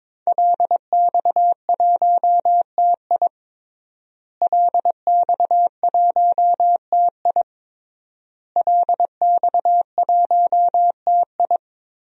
Morse code, heard as a keyed tone, about 22 wpm, 700 Hz. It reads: LX1TI LX1TI LX1TI
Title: DXCC-100 Call Signs - 5 Characters - Repeated 3x - 22wpm